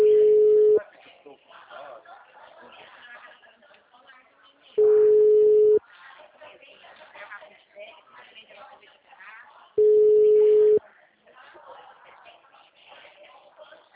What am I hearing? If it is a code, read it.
Ringback tone (Europe, 425 Hz), with speech.